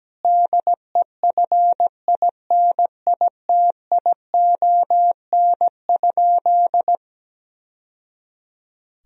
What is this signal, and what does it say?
Morse code, heard as a keyed tone, about 17 wpm, 700 Hz. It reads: DEFINITION?
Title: Sentences from Top 1000 Words - Review Part 4 - 17wpm